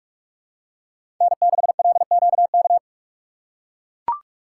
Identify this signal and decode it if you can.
Morse code, heard as a keyed tone, about 41 wpm, 700 Hz. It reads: N6LQK
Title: USA Call Signs - 5 Characters - 4x Character Spacing - 45wpm